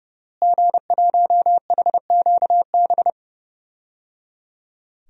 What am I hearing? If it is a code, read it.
Morse code, heard as a keyed tone, about 30 wpm, 700 Hz. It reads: G1HQB
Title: DXCC-100 Call Signs - 5 Characters - Repeated 3x - 30wpm